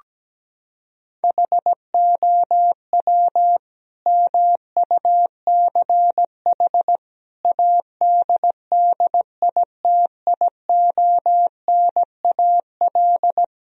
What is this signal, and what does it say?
Morse code, heard as a keyed tone, about 17 wpm, 700 Hz. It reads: HOW MUCH ADDITIONAL
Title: Sentences from Top 1000 Words - Review Part 4 - 17wpm